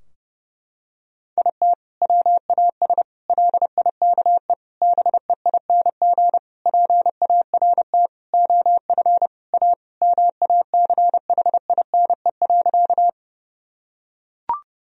Morse code, heard as a keyed tone, about 30 wpm, 700 Hz. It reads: IT WAS LIKE BEING PART OF A MACHINE.
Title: Sentences from Top 1000 Words - Review Part 3 - 30wpm